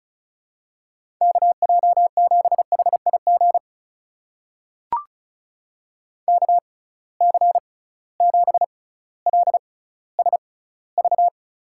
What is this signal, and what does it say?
Morse code, heard as a keyed tone, about 34 wpm, spaced as for 18 wpm, 700 Hz. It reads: KJ7HIG KC7LSV
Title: USA Call Signs - 6 Characters - 6x Character Spacing - 35wpm